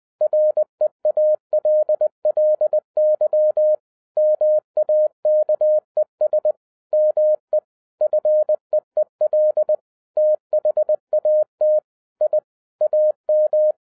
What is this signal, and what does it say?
Morse code, heard as a keyed tone, about 20 wpm, 610 Hz. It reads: REALLY MAKES ME FEEL THAT I AM
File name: 2012-01-mp3 / n05-CAME BACK TO HAM RADIO AFTER 30 YEARS OF INTERVAL-20wpm